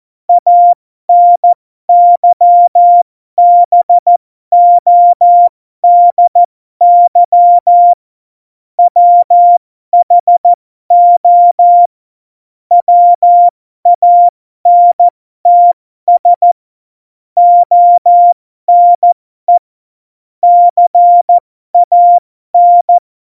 Morse code, heard as a keyed tone, about 14 wpm, spaced as for 12 wpm, 700 Hz. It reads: ANYBODY WHO WANTS ONE CAN
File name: Morsekurs / Witze